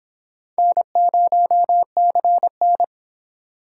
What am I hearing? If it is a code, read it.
Morse code, heard as a keyed tone, about 22 wpm, 700 Hz. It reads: N0CN